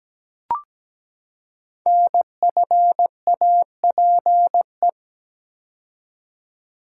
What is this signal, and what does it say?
Morse code, heard as a keyed tone, about 17 wpm, 700 Hz. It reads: NFAPE
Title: Code Groups - 5 Letters - 17wpm